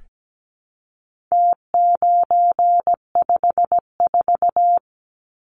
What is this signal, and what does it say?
Morse code, heard as a keyed tone, about 17 wpm, 700 Hz. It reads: T954
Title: Numbers - Contest Style - 4-Digit 17wpm